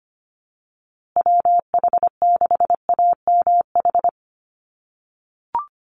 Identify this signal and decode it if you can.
Morse code, heard as a keyed tone, about 25 wpm, 700 Hz. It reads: WH6AMH
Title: USA Call Signs Repeated 3x 25wpm